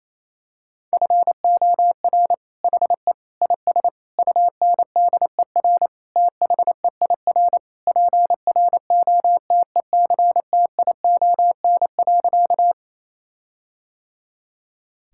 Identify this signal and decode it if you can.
Morse code, heard as a keyed tone, about 28 wpm, 700 Hz. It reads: FOR HE IS UNDER THEIR PROTECTION.
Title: Sentences from Top 1600 Words 28wpm